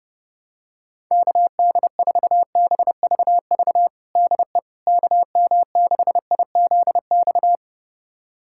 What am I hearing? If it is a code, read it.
Morse code, heard as a keyed tone, about 30 wpm, 700 Hz. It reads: KD4BVV DE KM6IZX
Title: USA Callsign DE Callsign - 6 Characters - 30wpm